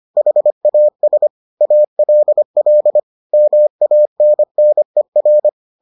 Morse code, heard as a keyed tone, about 25 wpm, 600 Hz. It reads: HAS ALL MANNER